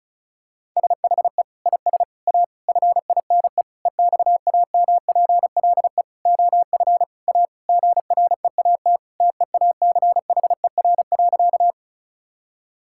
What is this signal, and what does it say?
Morse code, heard as a keyed tone, about 34 wpm, 700 Hz. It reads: SHE IS A FINE EXAMPLE OF A GREAT TEACHER.
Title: Sentences from Top 1000 Words - Review Part 6 - 35wpm